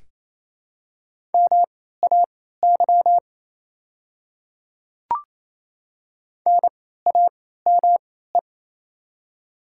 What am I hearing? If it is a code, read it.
Morse code, heard as a keyed tone, about 28 wpm, spaced as for 16 wpm, 700 Hz. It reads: MAY NAME
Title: 3x Character Spacing - Top 200 Words - 28wpm